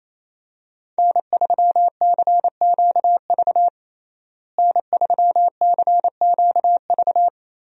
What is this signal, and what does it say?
Morse code, heard as a keyed tone, about 28 wpm, 700 Hz. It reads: N3CQV N3CQV
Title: USA Call Signs - 5 Characters - Repeated 2x - 28wpm